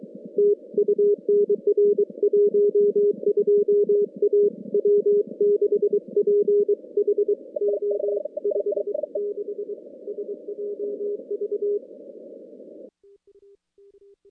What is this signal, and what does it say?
Morse code, heard as a keyed tone, about 23 wpm, 410 Hz. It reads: TU NR 12 A W6PHO 56 SJV
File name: contests / CW-SS-k5tr- -2100